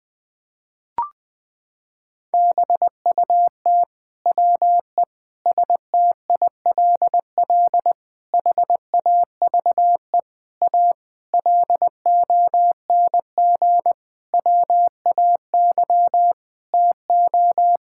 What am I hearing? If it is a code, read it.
Morse code, heard as a keyed tone, about 20 wpm, 700 Hz. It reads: BUT WE STILL HAVE A LONG WAY TO
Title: Sentences with 9-Word Phrases - 20wpm